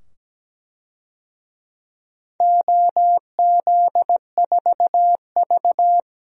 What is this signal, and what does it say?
Morse code, heard as a keyed tone, about 17 wpm, 700 Hz. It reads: OZ4V
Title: DXCC-100 Call Signs - 4 Characters - 17wpm